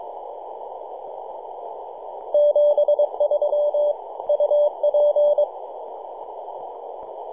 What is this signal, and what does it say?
Morse code, heard as a keyed tone, about 23 wpm, 590 Hz. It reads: 73 UP
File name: 3d2ya-18-085-cw